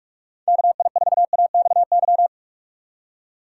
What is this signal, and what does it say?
Morse code, heard as a keyed tone, about 45 wpm, 700 Hz. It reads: KI4AXY
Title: USA Call Signs - 6 Characters - 3x Character Spacing - 45wpm